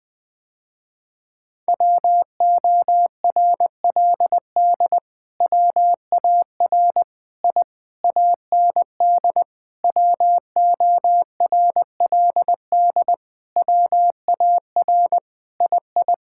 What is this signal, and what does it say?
Morse code, heard as a keyed tone, about 21 wpm, 700 Hz. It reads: WORLD WAR I AND WORLD WAR II
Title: Sentences with 7-Word Phrases - Spoken Beginning - 20wpm